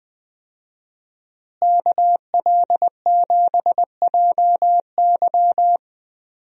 Morse code, heard as a keyed tone, about 20 wpm, 700 Hz. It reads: KL7JY
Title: DXCC-100 Call Signs - 5 Characters - Repeated 2x - 20wpm